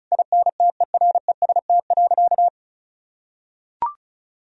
Morse code, heard as a keyed tone, about 36 wpm, 700 Hz. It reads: INTEREST.
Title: Sentences from Top 1000 Words - Review Part 4 - 35wpm